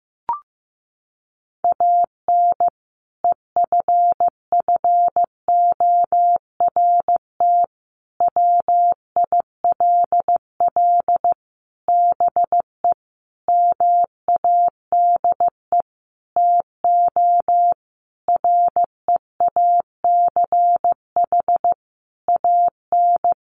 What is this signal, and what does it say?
Morse code, heard as a keyed tone, about 15 wpm, 700 Hz. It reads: AN EFFORT WILL BE MADE TO REACH AN
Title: Sentences from Top 1500 Words 15wpm